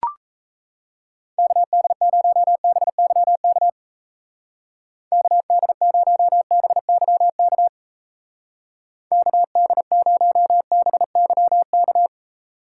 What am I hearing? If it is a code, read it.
Morse code, heard as a keyed tone, about 35 wpm, 700 Hz. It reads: KD0BYK KD0BYK KD0BYK
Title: Speed Racing - USA 2x3 Call Signs 28wpm